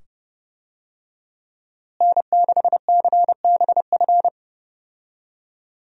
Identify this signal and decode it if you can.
Morse code, heard as a keyed tone, about 29 wpm, 700 Hz. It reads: N6CBF
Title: USA Call Signs - 5 Characters - Repeated 2x - 30wpm